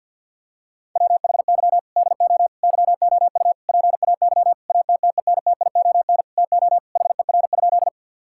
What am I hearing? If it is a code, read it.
Morse code, heard as a keyed tone, about 50 wpm, 700 Hz. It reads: WHY DO YOU PAY ATTENTION TO HER?
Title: Sentences from Top 1000 Words - Review Part 6 - 50wpm